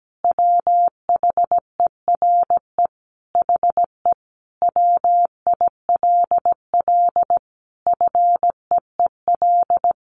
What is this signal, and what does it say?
Morse code, heard as a keyed tone, about 17 wpm, 700 Hz. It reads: WHERE HE WILL FEEL
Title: Sentences from Top 1500 Words 17wpm